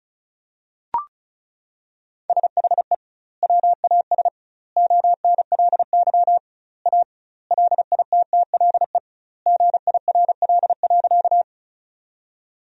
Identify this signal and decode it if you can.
Morse code, heard as a keyed tone, about 36 wpm, 700 Hz. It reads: SHE WAS ONLY A LITTLE GIRL.
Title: 2x Word-Spacing - Sentences from Top 400 Words - 35wpm